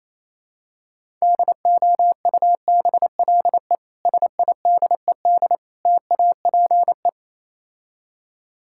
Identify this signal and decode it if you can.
Morse code, heard as a keyed tone, about 29 wpm, 700 Hz. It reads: DOUBLE SIDED TAPE